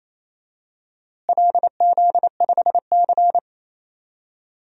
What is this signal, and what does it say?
Morse code, heard as a keyed tone, about 27 wpm, 700 Hz. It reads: LZ5C
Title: DXCC-100 Call Signs - 4 Characters - Repeated 3x - 28wpm